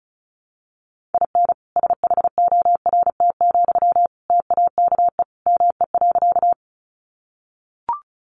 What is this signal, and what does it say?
Morse code, heard as a keyed tone, about 34 wpm, 700 Hz. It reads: IN SHORT, TAKE ME.
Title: Sentences from Top 400 Words 35wpm